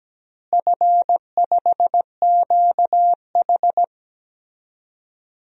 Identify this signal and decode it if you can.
Morse code, heard as a keyed tone, about 17 wpm, 700 Hz. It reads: F5QH
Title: DXCC-100 Call Signs - 4 Characters - Repeated 3x - 17wpm